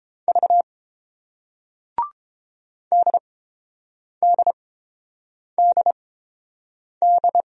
Morse code, heard as a keyed tone, about 29 wpm, spaced as for 7 wpm, 700 Hz. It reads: V DDDD